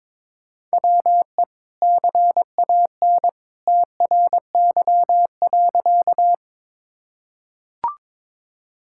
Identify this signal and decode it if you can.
Morse code, heard as a keyed tone, about 22 wpm, 700 Hz. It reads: WE CAN TRY.
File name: practice / 22wpm / Sentences with 2-Word Phrases - 22wpm